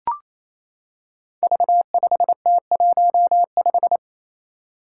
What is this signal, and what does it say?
Morse code, heard as a keyed tone, about 28 wpm, 700 Hz. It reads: V5T15
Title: Five Challenging Letter-Number Combinations 28wpm